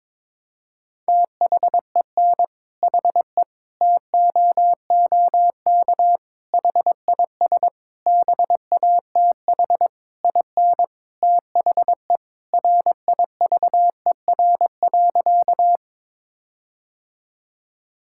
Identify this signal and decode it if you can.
Morse code, heard as a keyed tone, about 22 wpm, 700 Hz. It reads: THEN HE TOOK HIS BATH IN THE RIVER.
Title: Sentences from Top 1600 Words 22wpm